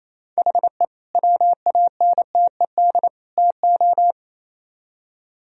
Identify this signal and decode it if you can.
Morse code, heard as a keyed tone, about 34 wpm, 700 Hz. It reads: HE WANTED TO
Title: Speed-Racing - Sets of 3 Words 28wpm